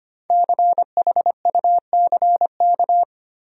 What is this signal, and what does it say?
Morse code, heard as a keyed tone, about 25 wpm, 700 Hz. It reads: CHUCK
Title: USA Common Names - Max 5 Letters - 25wpm